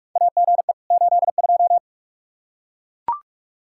Morse code, heard as a keyed tone, about 42 wpm, 700 Hz. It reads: AGE 82
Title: Practice QSO - Exchange 3 - More Info - 45wpm